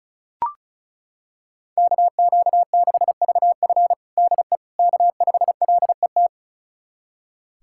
Morse code, heard as a keyed tone, about 36 wpm, 690 Hz. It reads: KQ6VF DE K5LET
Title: USA Callsign DE Callsign - 5 Characters - 35wpm